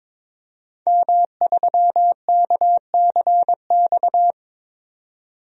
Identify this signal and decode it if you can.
Morse code, heard as a keyed tone, about 22 wpm, 700 Hz. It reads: M3KCX
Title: DXCC-100 Call Signs - 5 Characters - 22wpm